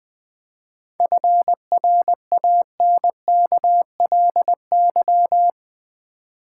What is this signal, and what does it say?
Morse code, heard as a keyed tone, about 20 wpm, 700 Hz. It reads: FRANKLY
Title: 7 Letter Words - First 3 Letters Given - 20wpm